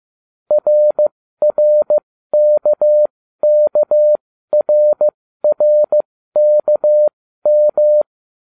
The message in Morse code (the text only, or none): RRKKRRKM